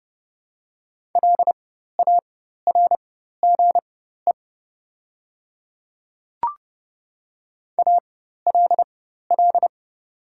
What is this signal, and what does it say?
Morse code, heard as a keyed tone, about 29 wpm, spaced as for 14 wpm, 690 Hz. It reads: LARGE ALL
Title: 4x Character Spacing - Top 500 Words - 30wpm